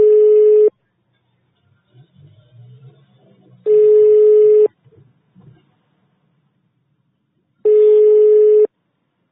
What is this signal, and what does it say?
Ringback tone (425 Hz, 1 s on, 3 s off).
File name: sauvegarde / ACRRecordings- -jusquau 17-juin-2017 / Police De Huy 2017 05 24 12 57 02